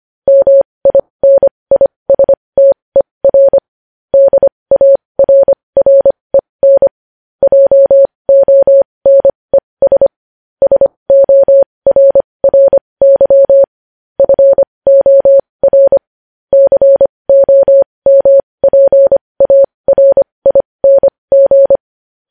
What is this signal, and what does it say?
Morse code, heard as a keyed tone, about 25 wpm, spaced as for 20 wpm, 550 Hz. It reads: MINISTER DARREN JONES SORRY FOR COMPARING
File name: podcast / News headlines-1x-25x20